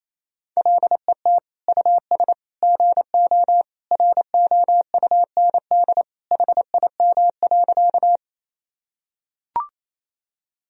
Morse code, heard as a keyed tone, about 28 wpm, 700 Hz. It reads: LET US GO ROUND HIM.
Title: Sentences from Top 1000 Words - Review Part 1 - 28wpm